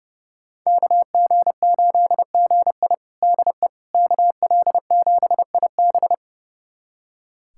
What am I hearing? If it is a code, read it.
Morse code, heard as a keyed tone, about 30 wpm, 700 Hz. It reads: KG8GI DE KL7IB
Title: USA Callsign DE Callsign 30wpm